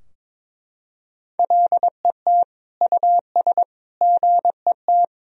Morse code, heard as a keyed tone, about 22 wpm, 700 Hz. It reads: LET US GET
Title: Sentences from Top 1000 Words - Review Part 3 - 22wpm